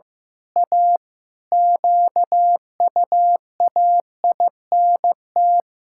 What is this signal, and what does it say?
Morse code, heard as a keyed tone, about 15 wpm, 700 Hz. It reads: A QUAINT
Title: Pangram Challenge v2 - 15wpm